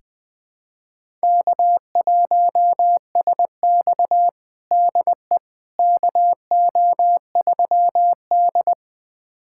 Morse code, heard as a keyed tone, about 20 wpm, 700 Hz. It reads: K1SX DE KO3D
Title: USA Callsign DE Callsign - 4 Characters - 20wpm